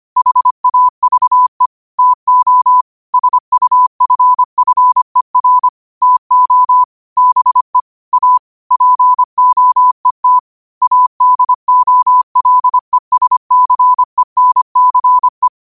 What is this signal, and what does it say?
Morse code, heard as a keyed tone, about 25 wpm, 1000 Hz. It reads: HAVE TO SUFFER TO BE A POET ADOLESCENCE